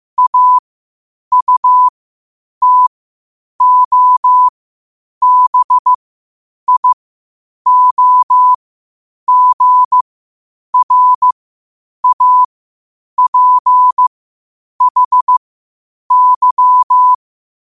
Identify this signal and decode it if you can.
Morse code, heard as a keyed tone, about 15 wpm, spaced as for 8 wpm, 1000 Hz. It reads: AUTOBIOGRAPHY
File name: qotdfiles / 8wpm